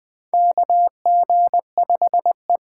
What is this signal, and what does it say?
Morse code, heard as a keyed tone, about 20 wpm, 700 Hz. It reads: KG5E
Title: USA Call Signs - 6 Characters - 8x Character Spacing - 20wpm